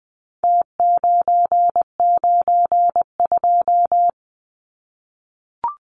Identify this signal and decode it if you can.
Morse code, heard as a keyed tone, about 20 wpm, 700 Hz. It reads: T992
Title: Numbers - Contest Style - 4-Digit 20wpm